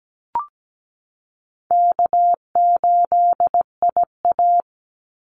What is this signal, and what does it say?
Morse code, heard as a keyed tone, about 17 wpm, 700 Hz. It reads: K8IA
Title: Easy Call Signs Repeated 3x - 17wpm